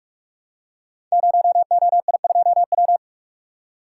Morse code, heard as a keyed tone, about 44 wpm, 700 Hz. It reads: 0OI2W
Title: Code Groups - 5 Letter:Numbers - 45wpm